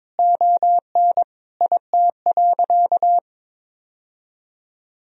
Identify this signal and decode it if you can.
Morse code, heard as a keyed tone, about 22 wpm, 700 Hz. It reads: ON IT.
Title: Sentences from Top 1000 Words - Review Part 1 - 22wpm